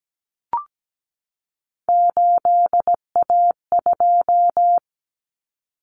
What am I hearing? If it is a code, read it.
Morse code, heard as a keyed tone, about 17 wpm, 700 Hz. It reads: 8A2
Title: Common Cut Numbers - 3-Digit 17wpm